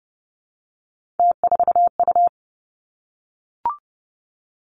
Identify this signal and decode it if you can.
Morse code, heard as a keyed tone, about 30 wpm, 700 Hz. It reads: T4U
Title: Three Challenging Letter-Number Combinations 30wpm